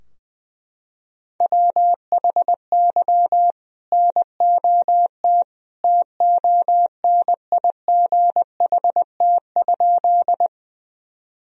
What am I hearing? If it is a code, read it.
Morse code, heard as a keyed tone, about 20 wpm, 700 Hz. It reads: WHY NOT TONIGHT?
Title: Sentences from Top 1000 Words - Review Part 3 - 20wpm